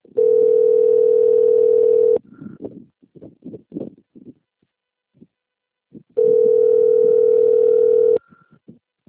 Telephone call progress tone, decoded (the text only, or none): ringback tone